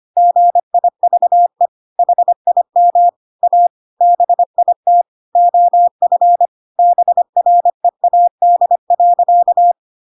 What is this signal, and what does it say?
Morse code, heard as a keyed tone, about 25 wpm, 700 Hz. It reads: GIVE HIM A BIT OF BREAD.
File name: Pinocchio / Pinocchio ch5 25 25 ik1hge